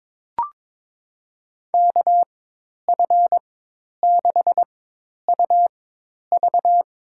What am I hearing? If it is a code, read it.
Morse code, heard as a keyed tone, about 21 wpm, spaced as for 10 wpm, 700 Hz. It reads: KF6UV
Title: USA Call Signs - 4x Character Spacing - 22wpm